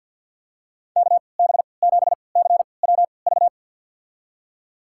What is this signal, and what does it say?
Morse code, heard as a keyed tone, about 47 wpm, spaced as for 28 wpm, 700 Hz. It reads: KB7CWV
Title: USA Call Signs - 3x Character Spacing - 50wpm